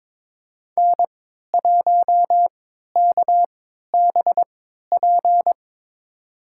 Morse code, heard as a keyed tone, about 22 wpm, spaced as for 13 wpm, 700 Hz. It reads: N1KBP
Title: USA Call Signs - 5 Characters - 3x Character Spacing - 22wpm